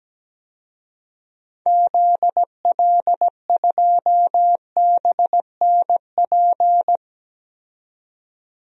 Morse code, heard as a keyed tone, about 17 wpm, 700 Hz. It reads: ZL2BNP
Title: DXCC-100 Call Signs - 6 Characters - 17wpm